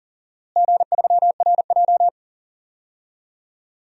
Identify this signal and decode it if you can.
Morse code, heard as a keyed tone, about 37 wpm, 700 Hz. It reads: G3RJ
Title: DXCC-100 Call Signs - 4 Characters - Repeated 2x - 40wpm